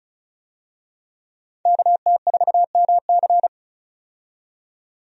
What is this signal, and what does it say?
Morse code, heard as a keyed tone, about 34 wpm, 700 Hz. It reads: KT4MC
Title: USA Call Signs - 5 Characters - 35wpm